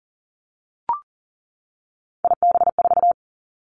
Morse code, heard as a keyed tone, about 40 wpm, 690 Hz. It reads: IB4
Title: Three Challenging Letter-Number Combinations 40wpm